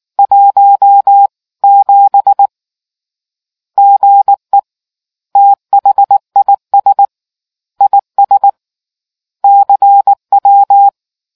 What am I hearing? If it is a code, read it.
Morse code, heard as a keyed tone, about 19 wpm, 800 Hz. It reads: 17 GE THIS IS CW